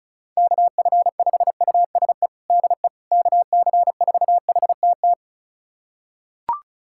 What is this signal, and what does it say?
Morse code, heard as a keyed tone, about 36 wpm, 700 Hz. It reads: KF5USE DE KC4HTT
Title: USA Callsign DE Callsign 35wpm